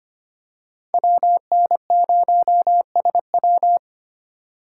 Morse code, heard as a keyed tone, about 25 wpm, 700 Hz. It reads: WN0SW
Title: USA Call Signs - 5 Characters - 2x Character Spacing - 25wpm